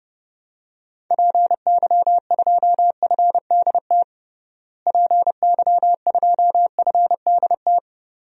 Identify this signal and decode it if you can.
Morse code, heard as a keyed tone, about 32 wpm, 700 Hz. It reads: PY2FDT PY2FDT
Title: DXCC-100 Call Signs - 6 Characters - Repeated 2x - 30wpm